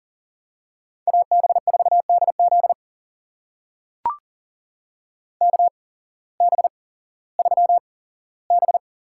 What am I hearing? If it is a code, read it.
Morse code, heard as a keyed tone, about 40 wpm, spaced as for 19 wpm, 700 Hz. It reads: AB4DZ KB3B
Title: USA Call Signs - 5 Characters - 8x Character Spacing - 40wpm